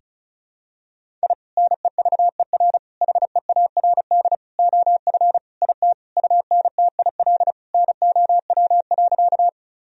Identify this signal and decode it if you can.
Morse code, heard as a keyed tone, about 36 wpm, 700 Hz. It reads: I NEVER HEARD OF IT UNTIL NOW.
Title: Sentences from Top 1000 Words - Review Part 1 - 35wpm